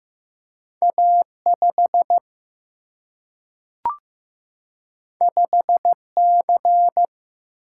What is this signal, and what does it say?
Morse code, heard as a keyed tone, about 15 wpm, 700 Hz. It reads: A 5 5 C
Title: Single-Letter Single-Number 15wpm